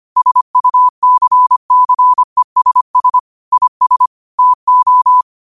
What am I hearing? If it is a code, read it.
Morse code, heard as a keyed tone, about 25 wpm, 1000 Hz. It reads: SUCCESS IS TO